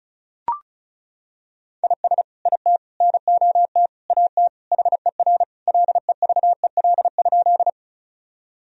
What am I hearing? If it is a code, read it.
Morse code, heard as a keyed tone, about 36 wpm, 700 Hz. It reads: IS IT NOT AT HER LEVEL?
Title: Sentences from Top 1000 Words - Review Part 4 - 35wpm